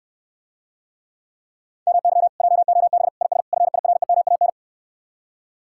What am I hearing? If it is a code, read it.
Morse code, heard as a keyed tone, about 69 wpm, 700 Hz. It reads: MY JOB IS FIREMAN